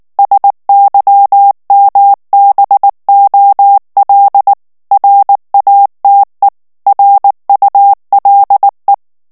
Morse code, heard as a keyed tone, about 19 wpm, 800 Hz. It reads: SYMBOL RATE RULE
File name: ANGLAIS texte clair 3 vitesse 18 mots minute